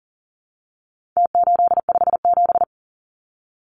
Morse code, heard as a keyed tone, about 40 wpm, 700 Hz. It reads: T857
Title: Numbers - Contest Style - 4-Digit 40wpm